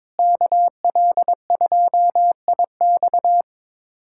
Morse code, heard as a keyed tone, about 22 wpm, 700 Hz. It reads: KL2IX
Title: USA Callsign DE Callsign - 5 Characters - 22wpm